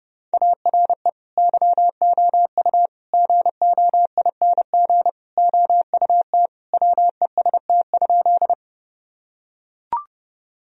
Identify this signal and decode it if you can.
Morse code, heard as a keyed tone, about 30 wpm, 700 Hz. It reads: ARE YOU GOING OUT WEST?
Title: Sentences from Top 1000 Words - Review Part 2 - 30wpm